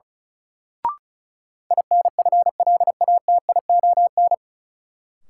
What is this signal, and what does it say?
Morse code, heard as a keyed tone, about 36 wpm, 700 Hz. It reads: INFLATION
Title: Sentences from Top 1100 Words - New Word Drill - 35wpm